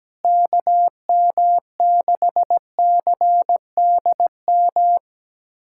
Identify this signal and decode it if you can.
Morse code, heard as a keyed tone, about 17 wpm, 700 Hz. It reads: KM6CDM
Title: USA Callsign DE Callsign 17wpm